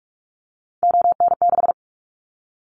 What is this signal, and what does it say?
Morse code, heard as a keyed tone, about 44 wpm, 700 Hz. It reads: ON6
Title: Callsign Prefixes for Top 100 DXCC - 45wpm